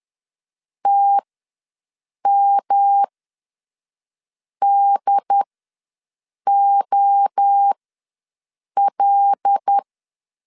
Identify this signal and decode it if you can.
Morse code, heard as a keyed tone, about 10 wpm, spaced as for 5 wpm, 800 Hz. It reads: TMDOL
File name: LEZIONI / 5 LBVC / 5 L ws 10wpm 3636